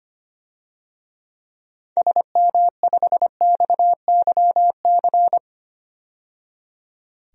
Morse code, heard as a keyed tone, about 25 wpm, 700 Hz. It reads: SM5XYC
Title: DXCC-100 Call Signs - 6 Characters - 25wpm